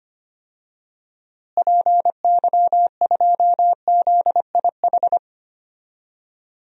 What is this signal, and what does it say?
Morse code, heard as a keyed tone, about 25 wpm, 700 Hz. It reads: PY2ZIH
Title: DXCC-100 Call Signs - 6 Characters - Repeated 3x - 25wpm